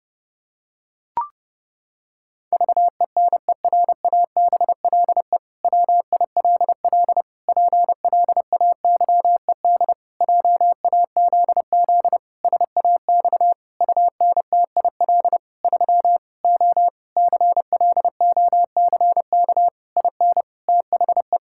Morse code, heard as a keyed tone, about 30 wpm, 700 Hz. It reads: VENERABLE WILL PLAYED JAZZ SAX UNTIL 3 O CLOCK IN THE
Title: Pangram Challenge v2 - 30wpm